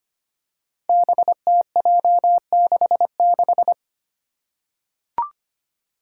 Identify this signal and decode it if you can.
Morse code, heard as a keyed tone, about 25 wpm, 700 Hz. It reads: BTJ66
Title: Five Challenging Letter-Number Combinations 25wpm